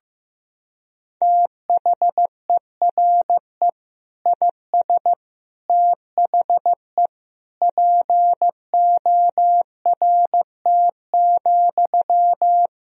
Morse code, heard as a keyed tone, about 15 wpm, 700 Hz. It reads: THERE IS THE PORT,
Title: Sentences from Top 1000 Words - Review Part 1 - 15wpm